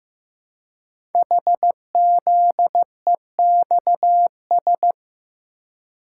Morse code, heard as a keyed tone, about 15 wpm, 700 Hz. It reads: HZEXS
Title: Code Groups - 5 Letters - 15wpm